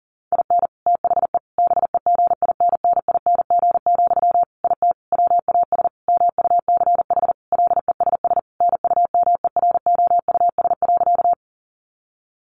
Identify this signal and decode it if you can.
Morse code, heard as a keyed tone, about 40 wpm, 700 Hz. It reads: IN THE BEGINNING, IT WAS MUCH LESS NUMEROUS.
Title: Sentences from Top 1400 Words 40wpm